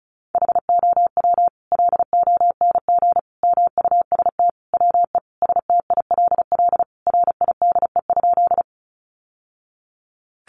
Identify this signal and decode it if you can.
Morse code, heard as a keyed tone, about 34 wpm, 700 Hz. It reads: HOW LONG MUST WE STILL RIDE?
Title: Sentences from Top 1200 Words 35wpm